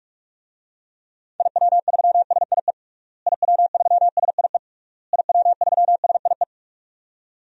Morse code, heard as a keyed tone, about 41 wpm, 700 Hz. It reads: IW3SIE IW3SIE IW3SIE
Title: DXCC-100 Call Signs - 6 Characters - Repeated 3x - 45wpm